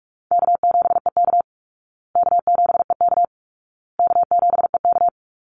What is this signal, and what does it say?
Morse code, heard as a keyed tone, about 44 wpm, 700 Hz. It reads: K7EK K7EK K7EK
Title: Easy Call Signs Repeated 3x - 45wpm